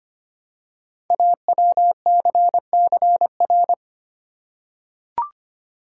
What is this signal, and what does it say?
Morse code, heard as a keyed tone, about 25 wpm, 700 Hz. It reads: AWCCR
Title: Code Groups - 5 Letters - 25wpm